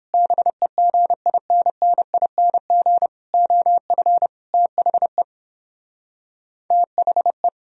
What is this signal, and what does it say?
Morse code, heard as a keyed tone, about 31 wpm, 700 Hz. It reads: BEGINNING OF THE THE